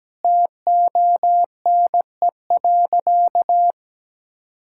Morse code, heard as a keyed tone, about 17 wpm, 700 Hz. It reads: TONE.
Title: Sentences from Top 1600 Words 17wpm